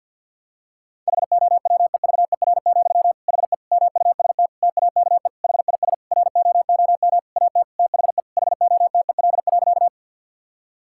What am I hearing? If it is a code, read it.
Morse code, heard as a keyed tone, about 50 wpm, 700 Hz. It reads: HOWEVER, HE MUST TAKE HIS ROOM AT THE HOTEL.
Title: Sentences from Top 1600 Words 50wpm